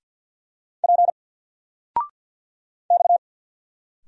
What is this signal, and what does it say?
Morse code, heard as a keyed tone, about 50 wpm, 700 Hz. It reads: P X
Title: Single Character (Letter:Number:Punctuation:Prosign) 50wpm